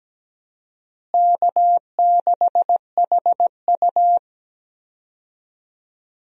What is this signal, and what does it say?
Morse code, heard as a keyed tone, about 17 wpm, 700 Hz. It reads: K6HU
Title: USA Call Signs - 4 Characters - Repeated 3x - 17wpm